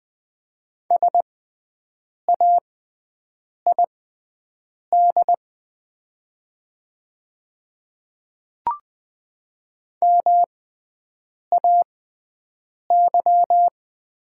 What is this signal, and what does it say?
Morse code, heard as a keyed tone, about 20 wpm, spaced as for 7 wpm, 700 Hz. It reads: SAID MAY
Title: 6x Character Spacing - Top 200 Words - 20wpm